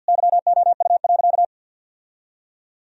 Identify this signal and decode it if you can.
Morse code, heard as a keyed tone, about 50 wpm, 700 Hz. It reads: YOU.
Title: Sentences from Top 1000 Words - Review Part 4 - 50wpm